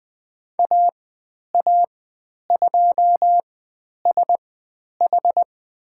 Morse code, heard as a keyed tone, about 20 wpm, spaced as for 10 wpm, 700 Hz. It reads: AA2SH
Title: 109 Lesson 18g - Letter m - Call Signs 20wpm